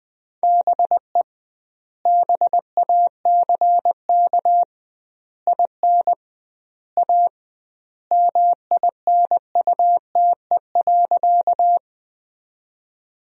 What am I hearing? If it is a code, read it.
Morse code, heard as a keyed tone, about 20 wpm, 700 Hz. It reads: BE BACK IN A MINUTE.
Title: 2x Word-Spacing - Sentences from Top 500 Words - 20wpm